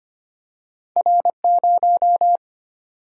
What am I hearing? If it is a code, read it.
Morse code, heard as a keyed tone, about 25 wpm, 700 Hz. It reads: R0
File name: Two Letter-Number Combinations - 25wpm